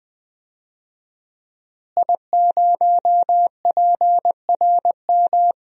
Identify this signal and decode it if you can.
Morse code, heard as a keyed tone, about 20 wpm, 700 Hz. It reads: I0PRM
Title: DXCC-100 Call Signs - 5 Characters - Repeated 3x - 20wpm